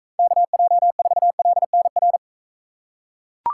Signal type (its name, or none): Morse code